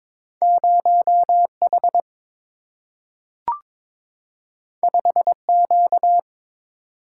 Morse code, heard as a keyed tone, about 22 wpm, 700 Hz. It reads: 0 H 5 Q
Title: Single-Letter Single-Number 22wpm